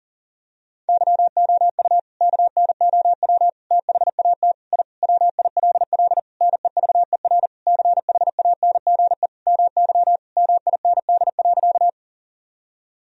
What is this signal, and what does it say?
Morse code, heard as a keyed tone, about 40 wpm, 700 Hz. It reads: YOU KNOW THAT I WILL NEVER CHANGE MY MIND.
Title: Sentences from Top 1000 Words - Review Part 1 - 40wpm